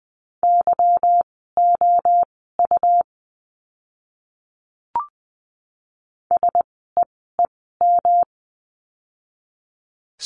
Morse code, heard as a keyed tone, about 20 wpm, spaced as for 15 wpm, 700 Hz. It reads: YOU SEEM
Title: 237 Lesson 38f - Letter q - Words 20wpm